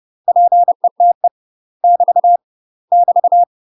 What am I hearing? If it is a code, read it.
Morse code, heard as a keyed tone, about 30 wpm, 700 Hz. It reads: PETE <BT> <BT>
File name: UK MASTER QSOs TRAINING / 30WPM / UK MASTER QSOs 0017